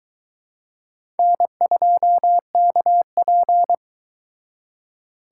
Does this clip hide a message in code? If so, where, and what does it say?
Yes: Morse code from 1.2 to 3.7 s, heard as a keyed tone, about 22 wpm, 700 Hz. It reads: N2KP